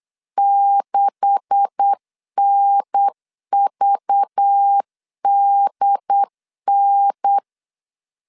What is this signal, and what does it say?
Morse code, heard as a keyed tone, about 8 wpm, 800 Hz. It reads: 6NVDN
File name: CW / DET / 1 9 1500C S C 8wpm 5653